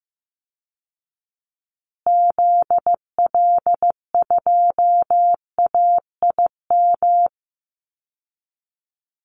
Morse code, heard as a keyed tone, about 15 wpm, 700 Hz. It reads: ZL2AIM
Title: Call Signs Repeated 2x - 15wpm